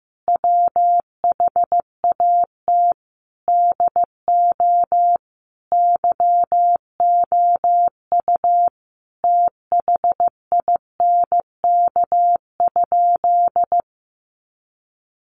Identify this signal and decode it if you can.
Morse code, heard as a keyed tone, about 15 wpm, 700 Hz. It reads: WHAT DO YOU THINK?
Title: Sentences from Top 200 Words 15wpm